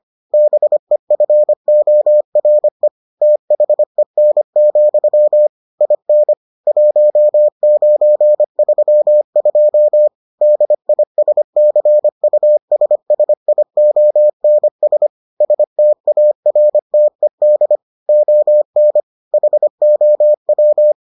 Morse code, heard as a keyed tone, about 25 wpm, 600 Hz. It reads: BEFORE THEN, IN 1932 DISCUSSIONS STARTED ON HOW